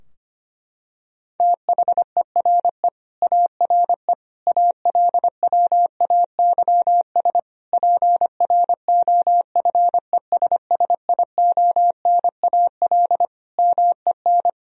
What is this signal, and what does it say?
Morse code, heard as a keyed tone, about 25 wpm, 700 Hz. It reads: THERE ARE ALWAYS PROFESSIONAL MEN
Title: Sentences from Top 1000 Words - Review Part 4 - 25wpm